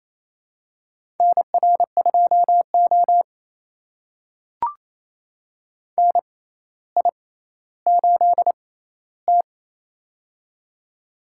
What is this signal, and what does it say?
Morse code, heard as a keyed tone, about 28 wpm, spaced as for 14 wpm, 700 Hz. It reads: NR2O NI8T
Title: USA Call Signs - 6x Character Spacing - 28wpm